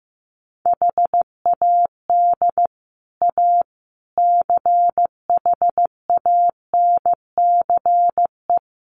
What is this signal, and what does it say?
Morse code, heard as a keyed tone, about 15 wpm, 700 Hz. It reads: HAD A CHANCE
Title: Sets of 4 Words 15wpm